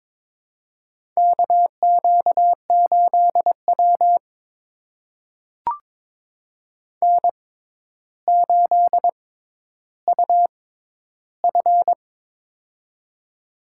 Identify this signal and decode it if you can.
Morse code, heard as a keyed tone, about 22 wpm, spaced as for 11 wpm, 700 Hz. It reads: KQ8W N8UF
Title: USA Call Signs - 4 Characters - 6x Character Spacing - 22wpm